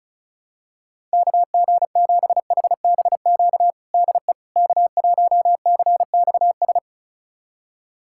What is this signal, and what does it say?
Morse code, heard as a keyed tone, about 36 wpm, 700 Hz. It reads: KG7HBQ DE K1CXS
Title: USA Callsign DE Callsign 35wpm